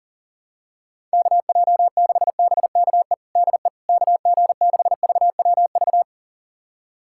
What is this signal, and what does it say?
Morse code, heard as a keyed tone, about 40 wpm, 700 Hz. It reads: KJ6BKE DE KG6VWV
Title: USA Callsign DE Callsign - 6 Characters - 40wpm